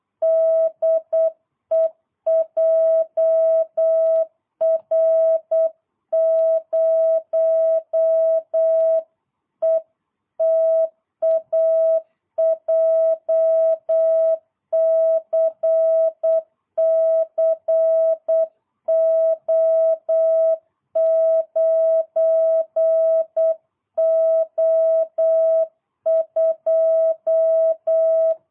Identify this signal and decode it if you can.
Morse code, heard as a keyed tone, about 8 wpm, 650 Hz. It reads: DE JR0ETA JCCO9O2
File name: jh8cbh / 2022cw